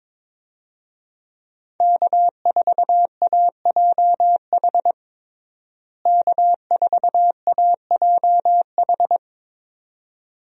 Morse code, heard as a keyed tone, about 22 wpm, 700 Hz. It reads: K4AJH K4AJH
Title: USA Call Signs - 5 Characters - Repeated 2x - 22wpm